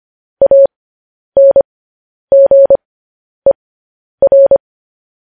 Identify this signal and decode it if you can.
Morse code, heard as a keyed tone, about 25 wpm, spaced as for 10 wpm, 550 Hz. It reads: ANGER
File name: podcast / News headlines-1x-25x10